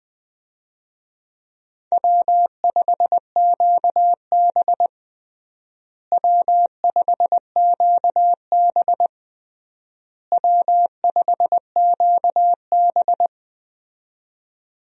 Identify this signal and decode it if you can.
Morse code, heard as a keyed tone, about 20 wpm, 700 Hz. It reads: W5QB W5QB W5QB
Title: USA Call Signs - 4 Characters - Repeated 3x - 20wpm